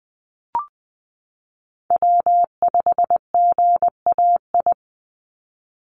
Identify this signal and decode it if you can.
Morse code, heard as a keyed tone, about 20 wpm, 700 Hz. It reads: W5GAI
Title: Call Signs Repeated 2x - 20wpm